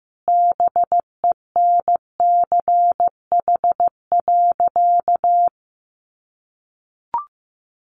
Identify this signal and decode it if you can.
Morse code, heard as a keyed tone, about 15 wpm, 700 Hz. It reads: BENCH.
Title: Sentences from Top 1400 Words 15wpm